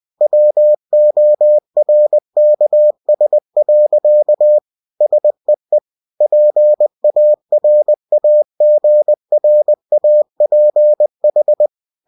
Morse code, heard as a keyed tone, about 20 wpm, 600 Hz. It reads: WORKS. SEE PARAGRAPH
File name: cw-books / winnie-the-pooh / winnie-20wpm-page-0113